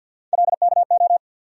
Morse code, heard as a keyed tone, about 48 wpm, 700 Hz. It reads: LKO
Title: USA Call Sign Postfixes - 3 Letters 50wpm